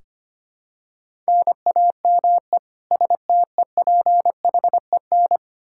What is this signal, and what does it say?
Morse code, heard as a keyed tone, about 25 wpm, 700 Hz. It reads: NAME STEPHEN
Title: Practice QSO - Exchange 1 - Beginning - 25wpm